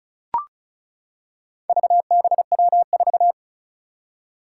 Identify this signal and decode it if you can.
Morse code, heard as a keyed tone, about 36 wpm, 700 Hz. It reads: VBW4
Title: Four Challenging Letter-Number Combinations 35wpm